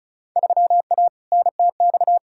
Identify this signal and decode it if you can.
Morse code, heard as a keyed tone, about 36 wpm, 700 Hz. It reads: 3A NTX
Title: Field Day Exchanges - 35wpm